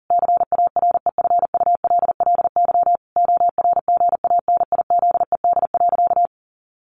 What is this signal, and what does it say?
Morse code, heard as a keyed tone, about 40 wpm, 700 Hz. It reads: CAREFULLY ORGANIZED.
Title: Sentences from Top 1300 Words 40wpm